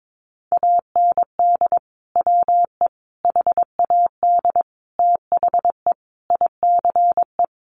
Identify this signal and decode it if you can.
Morse code, heard as a keyed tone, about 22 wpm, 700 Hz. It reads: AND WE HAD THE ICE
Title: Sentences from Top 1500 Words 22wpm